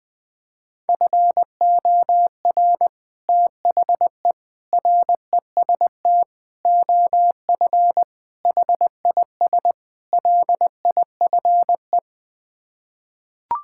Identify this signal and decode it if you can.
Morse code, heard as a keyed tone, about 20 wpm, 700 Hz. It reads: FOR THE REST OF HIS LIFE
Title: Sets of 6 Words - 20wpm